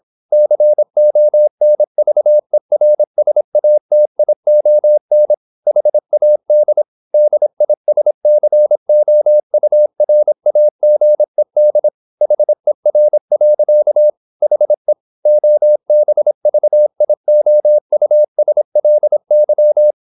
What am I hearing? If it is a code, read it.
Morse code, heard as a keyed tone, about 26 wpm, 600 Hz. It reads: CONVERSATION HAD DISCOURAGED HER. HE OBVIOUSLY
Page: Never 26wpm | CW med Gnister